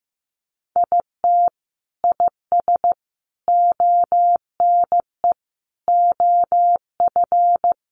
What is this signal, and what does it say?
Morse code, heard as a keyed tone, about 14 wpm, 700 Hz. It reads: IT IS ONE OF
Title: Sets of 5 Words 15wpm